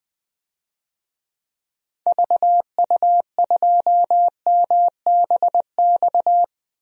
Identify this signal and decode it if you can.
Morse code, heard as a keyed tone, about 20 wpm, 700 Hz. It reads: VU2MBX
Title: DXCC-100 Call Signs - 6 Characters - 20wpm